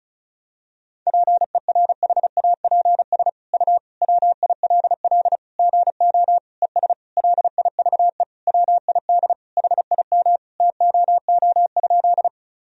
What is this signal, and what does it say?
Morse code, heard as a keyed tone, about 34 wpm, 700 Hz. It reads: PERHAPS U WILL GO ES LIVE WID HIM TOO?
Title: Sentences from Top 500 Words and Abbreviations 35wpm